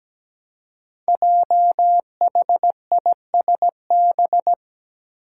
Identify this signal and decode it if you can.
Morse code, heard as a keyed tone, about 17 wpm, 700 Hz. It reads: JHISB
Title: Five Challenging Letter-Number Combinations 17wpm